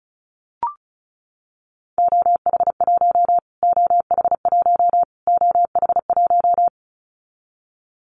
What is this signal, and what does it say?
Morse code, heard as a keyed tone, about 34 wpm, 700 Hz. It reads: OH1 OH1 OH1
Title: Callsign Prefixes for Top 25 DXCC - Repeated 3x - 35wpm